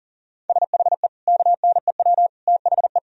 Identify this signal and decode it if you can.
Morse code, heard as a keyed tone, about 40 wpm, 700 Hz. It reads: SHE KNEW THE
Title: Sentences from Top 1000 Words - Review Part 4 - 40wpm